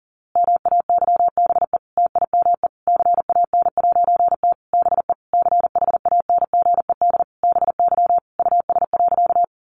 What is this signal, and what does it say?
Morse code, heard as a keyed tone, about 40 wpm, 700 Hz. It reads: MAYBE TIME CAN'T BE CHANGED BY US.
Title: A Sound Of Thunder 40wpm